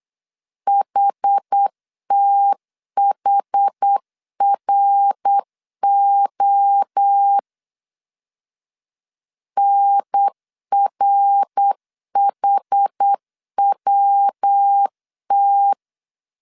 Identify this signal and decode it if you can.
Morse code, heard as a keyed tone, about 8 wpm, 800 Hz. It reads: HTHRO NRHWT
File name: CW / DET / 1 3 1500C S B 8wpm 4141